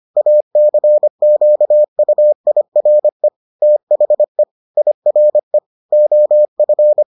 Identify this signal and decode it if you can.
Morse code, heard as a keyed tone, about 25 wpm, 600 Hz. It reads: ACQUIRE THE IRE OF